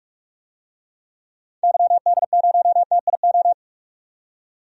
Morse code, heard as a keyed tone, about 46 wpm, 700 Hz. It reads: YD0TIO
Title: DXCC-100 Call Signs - 6 Characters - 45wpm